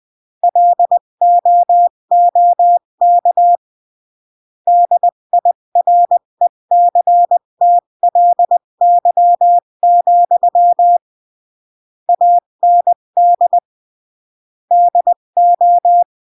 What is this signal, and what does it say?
Morse code, heard as a keyed tone, about 20 wpm, 700 Hz. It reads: LOOK DIRECTLY, AND DO
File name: CW / jodorowsky / 82 commandments 20-15